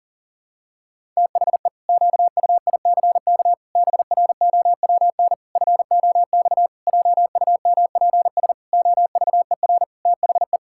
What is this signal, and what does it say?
Morse code, heard as a keyed tone, about 40 wpm, 700 Hz. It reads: THE QUICK BROWN FOX JUMPS OVER THE
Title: Pangram Challenge v2 - 40wpm